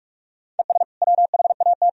Morse code, heard as a keyed tone, about 44 wpm, 700 Hz. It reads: ES WHAT
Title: Sentences from Top 200 Words and Abbreviations 45wpm